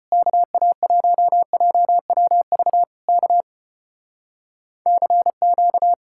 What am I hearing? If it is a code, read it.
Morse code, heard as a keyed tone, about 33 wpm, 700 Hz. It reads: KA1JWV K CQ